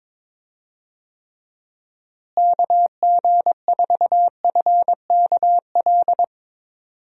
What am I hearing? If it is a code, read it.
Morse code, heard as a keyed tone, about 22 wpm, 700 Hz. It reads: KG4FKL
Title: USA Call Signs - 6 Characters - Repeated 2x - 22wpm